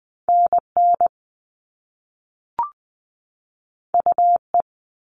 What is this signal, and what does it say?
Morse code, heard as a keyed tone, about 20 wpm, 700 Hz. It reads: NN UE
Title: All Cut Numbers - 2-Digit 20wpm